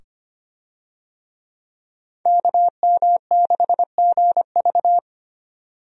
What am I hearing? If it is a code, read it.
Morse code, heard as a keyed tone, about 25 wpm, 700 Hz. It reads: KM6GV
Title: USA Call Signs - 5 Characters - 25wpm